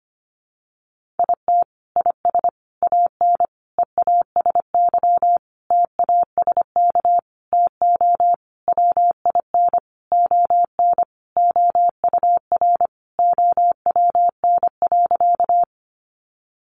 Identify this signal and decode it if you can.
Morse code, heard as a keyed tone, about 25 wpm, 700 Hz. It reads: IT IS AN EASY TASK TO WIN ON OUR OWN.
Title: Sentences from Top 1500 Words 25wpm